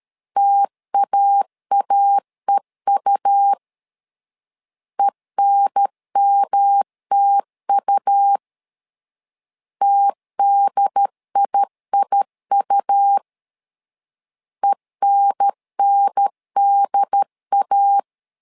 Morse code, heard as a keyed tone, about 13 wpm, 800 Hz. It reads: TAAEU ENMTU TDIIU ENNDA
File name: LEZIONI / 3 MWRO / 3 M 12wpm 1541